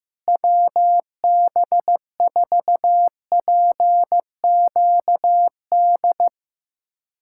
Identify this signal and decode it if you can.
Morse code, heard as a keyed tone, about 16 wpm, 700 Hz. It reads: WB4PQD
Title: USA Callsign DE Callsign - 6 Characters - 15wpm